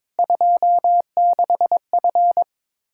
Morse code, heard as a keyed tone, about 23 wpm, 700 Hz. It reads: 26F
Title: Practice QSO - Exchange 2 - Info - 22wpm